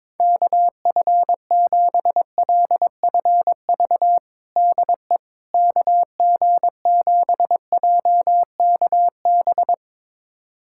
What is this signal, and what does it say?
Morse code, heard as a keyed tone, about 23 wpm, 700 Hz. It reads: KF7LFV DE KG7JKB
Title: USA Callsign DE Callsign 22wpm